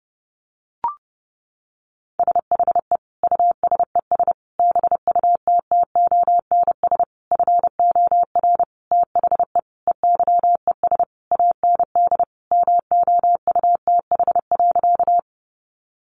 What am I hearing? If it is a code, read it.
Morse code, heard as a keyed tone, about 30 wpm, 700 Hz. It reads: SHE USES BUTTONS FOR THE EYES AND MOUTH.
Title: Easy Phrases 30wpm